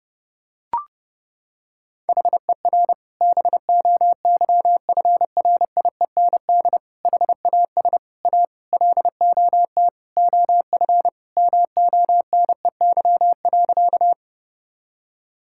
Morse code, heard as a keyed tone, about 30 wpm, 700 Hz. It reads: HER BOYFRIEND HAS A LOT OF MONEY.
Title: Sentences from Top 1000 Words - Review Part 5 - 30wpm